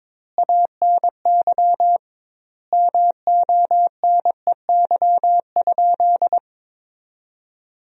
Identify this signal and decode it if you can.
Morse code, heard as a keyed tone, about 22 wpm, 700 Hz. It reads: ANY MONEY?
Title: 2x Word-Spacing - Sentences from Top 400 Words - 22wpm